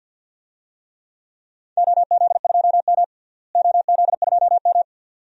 Morse code, heard as a keyed tone, about 47 wpm, 700 Hz. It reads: OZ2M OZ2M
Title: DXCC-100 Call Signs - 4 Characters - Repeated 2x - 50wpm